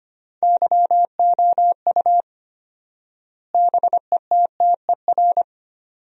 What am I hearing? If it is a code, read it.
Morse code, heard as a keyed tone, about 25 wpm, 700 Hz. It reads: YOU BETTER
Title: 4x Word-Spacing - Sentences from Top 500 Words - 25wpm